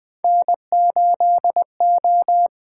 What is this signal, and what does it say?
Morse code, heard as a keyed tone, about 20 wpm, 700 Hz. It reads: N8O
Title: USA Call Signs - 4 Characters - 8x Character Spacing - 20wpm